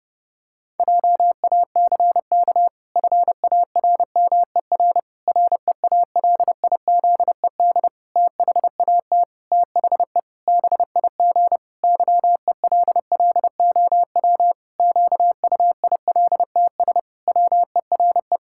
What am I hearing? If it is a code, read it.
Morse code, heard as a keyed tone, about 32 wpm, 700 Hz. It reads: JACK FARMER REALIZED THAT THE BIG YELLOW QUILTS WERE
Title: Pangram Challenge v2 - 30wpm